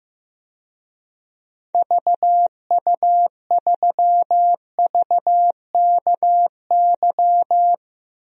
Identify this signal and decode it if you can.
Morse code, heard as a keyed tone, about 15 wpm, 700 Hz. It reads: VU3VKY
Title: DXCC-100 Call Signs - 6 Characters - Repeated 3x - 15wpm